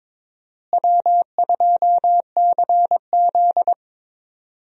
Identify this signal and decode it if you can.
Morse code, heard as a keyed tone, about 23 wpm, 700 Hz. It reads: W2CZ
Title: USA Call Signs - 4 Characters - Repeated 2x - 22wpm